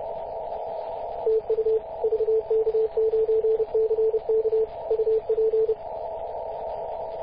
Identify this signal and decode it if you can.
Morse code, heard as a keyed tone, about 31 wpm, 440 Hz, with radio noise. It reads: TU VK9CK UP
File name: posts / 2016-mar-19-0241z-vk9ck-21-032mhz-cw